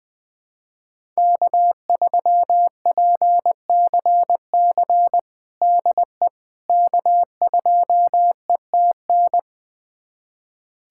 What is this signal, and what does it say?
Morse code, heard as a keyed tone, about 20 wpm, 700 Hz. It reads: K3PCC DE K2ETN
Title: USA Callsign DE Callsign - 5 Characters - 20wpm